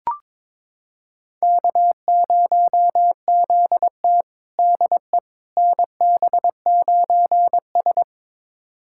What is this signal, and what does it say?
Morse code, heard as a keyed tone, about 21 wpm, 700 Hz. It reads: K0ZT DE NB9S
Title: USA Callsign DE Callsign - 4 Characters - 22wpm